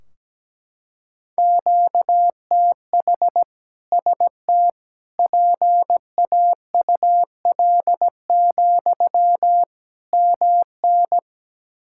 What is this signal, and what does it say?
Morse code, heard as a keyed tone, about 17 wpm, 700 Hz. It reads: QTH ST PAUL, MN
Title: K1USN SST QSO Practice 17wpm